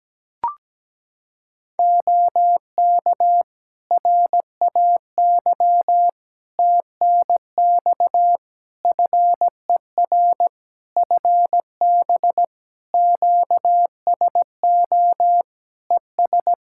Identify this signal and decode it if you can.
Morse code, heard as a keyed tone, about 17 wpm, 700 Hz. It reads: OK RAY TNX FER FB QSO ES
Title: Practice QSO - Exchange 4 - Ending - 17wpm